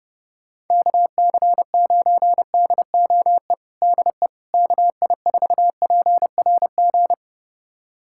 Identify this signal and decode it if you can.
Morse code, heard as a keyed tone, about 30 wpm, 700 Hz. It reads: KC9DOE DE KI4PRG
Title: USA Callsign DE Callsign - 6 Characters - 30wpm